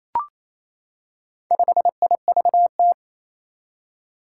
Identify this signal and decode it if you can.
Morse code, heard as a keyed tone, about 27 wpm, 700 Hz. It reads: 5IVT
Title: Four Challenging Letter-Number Combinations 28wpm